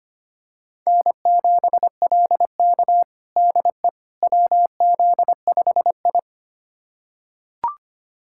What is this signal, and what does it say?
Morse code, heard as a keyed tone, about 25 wpm, 700 Hz. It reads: N7LK DE WZ5I
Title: USA Callsign DE Callsign - 4 Characters - 25wpm